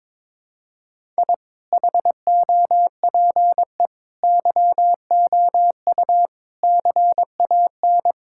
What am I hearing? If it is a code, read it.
Morse code, heard as a keyed tone, about 22 wpm, 700 Hz. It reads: I HOPE YOU CAN
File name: practice / 22wpm / Sentences with 8-Word Phrases - 22wpm